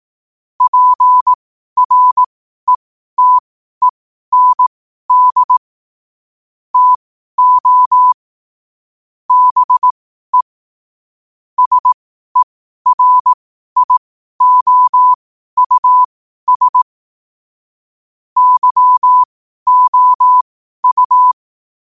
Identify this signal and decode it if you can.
Morse code, heard as a keyed tone, about 18 wpm, spaced as for 12 wpm, 1000 Hz. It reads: PRETEND TO BE SERIOUS YOU
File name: qotdfiles / 12wpm